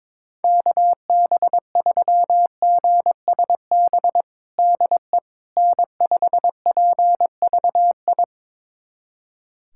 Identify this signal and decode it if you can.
Morse code, heard as a keyed tone, about 22 wpm, 700 Hz. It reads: KB3GSB DE N5PVI
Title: USA Callsign DE Callsign 22wpm